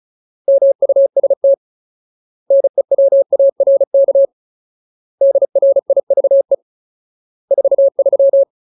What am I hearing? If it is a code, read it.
Morse code, heard as a keyed tone, about 34 wpm, 550 Hz. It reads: MUST NEWARK DRIVE 43
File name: Ultimate Word List / 35WPM Ultimate